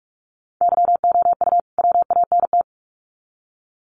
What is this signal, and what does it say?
Morse code, heard as a keyed tone, about 44 wpm, 700 Hz. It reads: YOU WANT
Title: Sets of 2 Words 45wpm